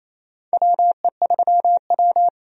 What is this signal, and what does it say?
Morse code, heard as a keyed tone, about 28 wpm, 700 Hz. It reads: WE3W
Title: USA Call Signs - 4 Characters - 28wpm